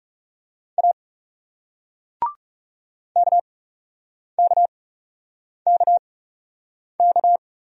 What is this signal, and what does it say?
Morse code, heard as a keyed tone, about 39 wpm, spaced as for 8 wpm, 700 Hz. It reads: A KKKK